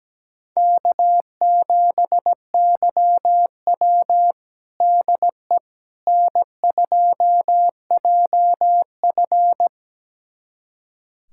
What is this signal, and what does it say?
Morse code, heard as a keyed tone, about 17 wpm, 700 Hz. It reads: K7YW DE N2JF
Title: USA Callsign DE Callsign - 4 Characters - 17wpm